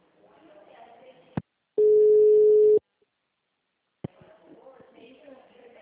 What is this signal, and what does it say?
Ringback tone (Europe, 425 Hz), with speech.